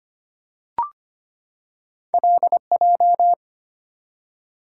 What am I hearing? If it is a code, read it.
Morse code, heard as a keyed tone, about 25 wpm, 690 Hz. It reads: LJ
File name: Two Letter-Number Combinations - 25wpm